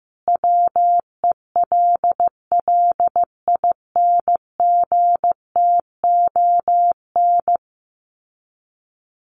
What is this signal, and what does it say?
Morse code, heard as a keyed tone, about 15 wpm, 700 Hz. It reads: WELLINGTON
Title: Common QSO Phrases 15wpm